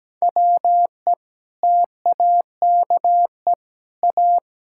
Morse code, heard as a keyed tone, about 17 wpm, 700 Hz. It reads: WE TAKE A
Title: Sentences from Top 1600 Words 17wpm